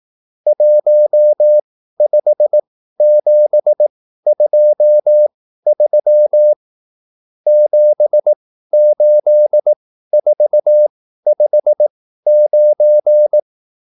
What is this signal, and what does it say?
Morse code, heard as a keyed tone, about 18 wpm, spaced as for 13 wpm, 600 Hz. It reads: 15723 78459